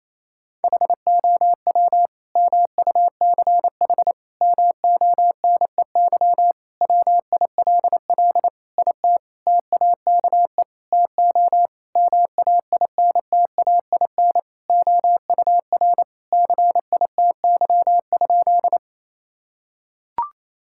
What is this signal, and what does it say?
Morse code, heard as a keyed tone, about 28 wpm, 700 Hz. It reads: HOW MUCH MONEY WILL IT TAKE TO MAINTAIN OUR CITY?
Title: Sentences from Top 1000 Words - Review Part 5 - 28wpm